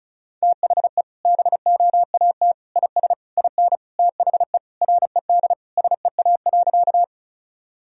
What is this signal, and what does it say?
Morse code, heard as a keyed tone, about 36 wpm, 700 Hz. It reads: THE BOAT IS IN THE RED SEA.
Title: Sentences from Top 1000 Words - Review Part 2 - 35wpm